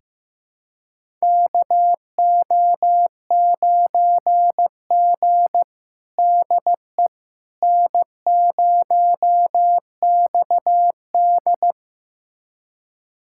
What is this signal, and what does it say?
Morse code, heard as a keyed tone, about 15 wpm, 700 Hz. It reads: KO9G DE N0XD
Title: USA Callsign DE Callsign - 4 Characters - 15wpm